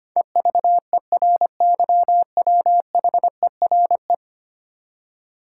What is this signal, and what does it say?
Morse code, heard as a keyed tone, about 25 wpm, 700 Hz. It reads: EVERYWHERE
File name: Ten Letter Words - 25wpm